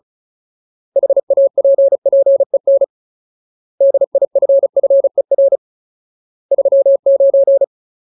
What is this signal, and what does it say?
Morse code, heard as a keyed tone, about 34 wpm, 550 Hz. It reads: HAPPEN DIFFER 39
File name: Ultimate Word List / 35WPM Ultimate